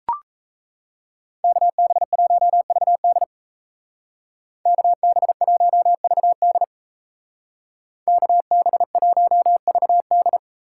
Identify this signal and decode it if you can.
Morse code, heard as a keyed tone, about 33 wpm, 700 Hz. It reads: KB1VD KB1VD KB1VD
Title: Speed Racing - USA 2x2 Call Signs 28wpm